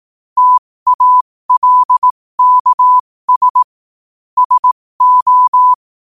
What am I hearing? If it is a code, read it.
Morse code, heard as a keyed tone, about 18 wpm, spaced as for 15 wpm, 1000 Hz. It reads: TALKS SO
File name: qotdfiles / 15wpm